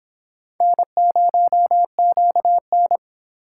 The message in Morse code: N0QN

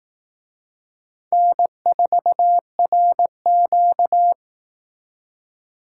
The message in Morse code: N4RQ

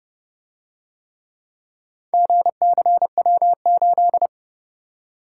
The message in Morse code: GCW8

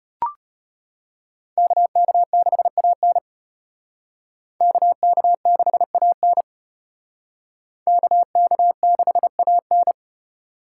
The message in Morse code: KK6AN KK6AN KK6AN

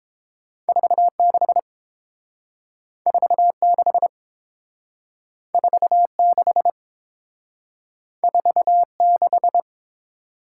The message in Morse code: 46 46 46 46

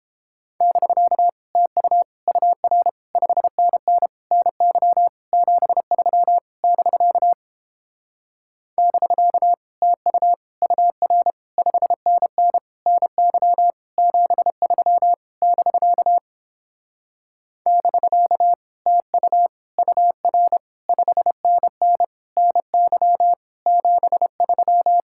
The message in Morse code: <BK> TU UR 5NN NY 73 <BK> <BK> TU UR 5NN NY 73 <BK> <BK> TU UR 5NN NY 73

